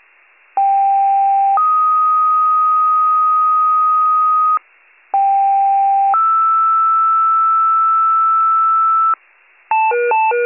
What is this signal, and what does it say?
Two-tone page, heard as a one-second tone, then a three-second tone, about 790 then 1250 Hz, about 790 then 1320 Hz. It is heard from 0.6 to 9.1 s.